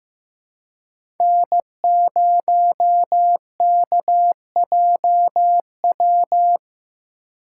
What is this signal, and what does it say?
Morse code, heard as a keyed tone, about 15 wpm, 700 Hz. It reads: N0KJW
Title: USA Call Signs - 5 Characters - 3x Character Spacing - 15wpm